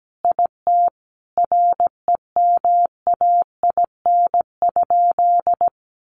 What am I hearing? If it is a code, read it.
Morse code, heard as a keyed tone, about 17 wpm, 700 Hz. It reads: IT REMAIN?
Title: Sentences from Top 1200 Words 17wpm